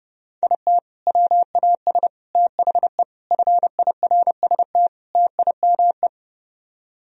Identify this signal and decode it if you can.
Morse code, heard as a keyed tone, about 40 wpm, spaced as for 35 wpm, 700 Hz. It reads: IT WAS THE FIRST TIME
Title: Speed-Racing - Sets of 5 Words 30wpm